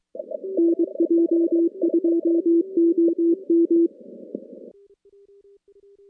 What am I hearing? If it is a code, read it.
Morse code, heard as a keyed tone, about 23 wpm, 340 Hz. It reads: NJ2OM
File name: contests / CW-SS-k5tr- -2200